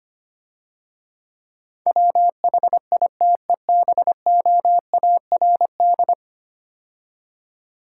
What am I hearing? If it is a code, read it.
Morse code, heard as a keyed tone, about 25 wpm, 700 Hz. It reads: WHITEBOARD
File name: Easy Words - Speed-Racing with Hints - 25wpm